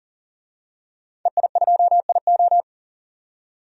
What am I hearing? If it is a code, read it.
Morse code, heard as a keyed tone, about 40 wpm, 700 Hz. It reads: EI2IO
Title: DXCC-100 Call Signs - 5 Characters - 40wpm